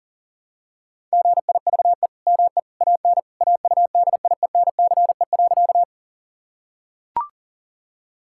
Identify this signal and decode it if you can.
Morse code, heard as a keyed tone, about 40 wpm, 700 Hz. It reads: GIVE ME AN AUDIENCE.
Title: Sentences from Top 1000 Words - Review Part 5 - 40wpm